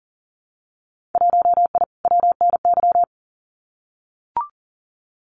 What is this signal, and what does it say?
Morse code, heard as a keyed tone, about 40 wpm, 700 Hz. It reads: 1I WNY
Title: Winter Field Day Exchanges Encore 40wpm